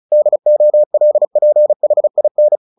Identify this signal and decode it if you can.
Morse code, heard as a keyed tone, about 34 wpm, 600 Hz. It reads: DOLPHIN
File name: the old man and the sea 2 35wpm